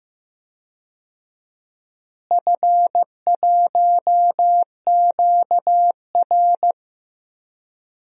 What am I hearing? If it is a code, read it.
Morse code, heard as a keyed tone, about 15 wpm, 700 Hz. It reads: F1QR
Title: DXCC-100 Call Signs - 4 Characters - Repeated 2x - 15wpm